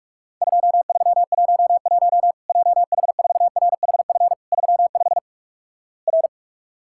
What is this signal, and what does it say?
Morse code, heard as a keyed tone, about 46 wpm, 700 Hz. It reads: 2311 JH4RHF 35 R
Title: Worked All Europe DX Contest - QTC Exchange - 45wpm